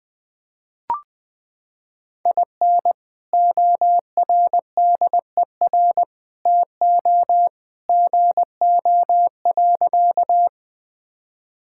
Morse code, heard as a keyed tone, about 20 wpm, 700 Hz. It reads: IN ORDER TO GO.
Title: Sentences with 2-Word Phrases - 20wpm